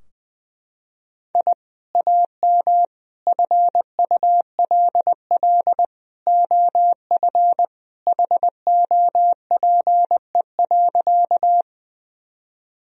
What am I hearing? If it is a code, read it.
Morse code, heard as a keyed tone, about 19 wpm, 700 Hz. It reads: I AM FULL OF HOPE.
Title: Sentences from Top 1000 Words - Review Part 3 - 20wpm